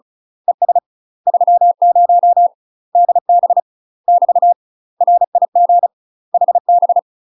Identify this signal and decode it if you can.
Morse code, heard as a keyed tone, about 35 wpm, 700 Hz. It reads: ES 30 DB <BT> RIG HB
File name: UK MASTER QSOs TRAINING / 35WPM / UK MASTER QSOs 0023